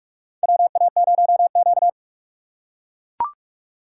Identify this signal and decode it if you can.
Morse code, heard as a keyed tone, about 44 wpm, 700 Hz. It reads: WA0Q
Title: USA Call Signs - 4 Characters - 2x Character Spacing - 45wpm